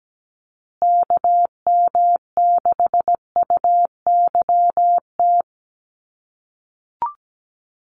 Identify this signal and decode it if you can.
Morse code, heard as a keyed tone, about 17 wpm, 700 Hz. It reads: KM6UYT
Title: USA Call Signs Repeated 2x 17wpm